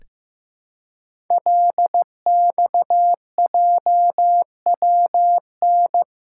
Morse code, heard as a keyed tone, about 15 wpm, 700 Hz. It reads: LXJWN
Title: Code Groups - 5 Letter:Numbers - 15wpm